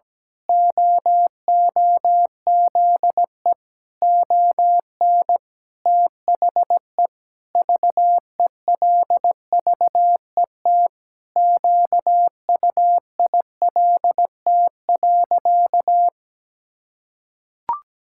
Morse code, heard as a keyed tone, about 17 wpm, 700 Hz. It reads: OOZE ON THE VELVET QUILT.
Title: Pangram Challenge v2 - 17wpm